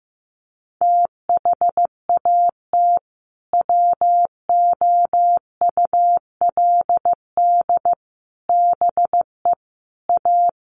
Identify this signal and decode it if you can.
Morse code, heard as a keyed tone, about 14 wpm, 700 Hz. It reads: THAT WOULD BE A
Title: Sentences from Top 500 Words 15wpm